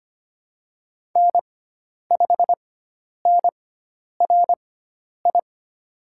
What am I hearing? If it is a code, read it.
Morse code, heard as a keyed tone, about 25 wpm, spaced as for 10 wpm, 700 Hz. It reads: N5NRI
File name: Lesson 16g - Letter c - Call Signs 25wpm